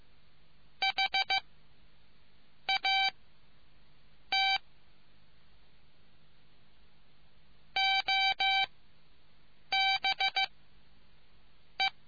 Morse code, heard as a keyed tone, about 15 wpm, spaced as for 6 wpm, 2300 Hz. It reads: HAT OBE